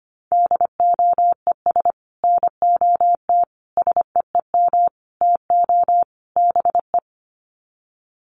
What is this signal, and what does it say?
Morse code, heard as a keyed tone, about 25 wpm, 700 Hz. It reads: DOES NOT SEEM TO BE
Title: Sets of 5 Words 25wpm